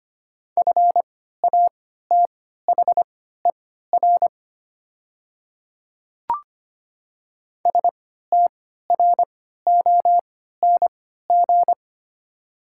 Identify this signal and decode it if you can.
Morse code, heard as a keyed tone, about 25 wpm, spaced as for 14 wpm, 700 Hz. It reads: FATHER STRONG
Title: 3x Character Spacing - Top 500 Words - 25wpm